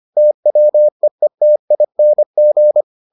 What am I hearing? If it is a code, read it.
Morse code, heard as a keyed tone, about 25 wpm, 600 Hz. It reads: TWEETING